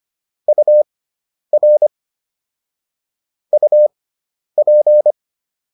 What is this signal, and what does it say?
Morse code, heard as a keyed tone, about 25 wpm, spaced as for 10 wpm, 600 Hz. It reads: UR UP